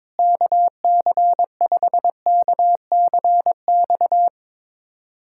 Morse code, heard as a keyed tone, about 21 wpm, 700 Hz. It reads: KC5KCX
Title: USA Callsign DE Callsign - 6 Characters - 22wpm